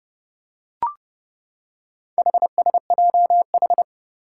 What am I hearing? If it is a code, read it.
Morse code, heard as a keyed tone, about 30 wpm, 700 Hz. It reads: HSJH
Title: Four Challenging Letter-Number Combinations 30wpm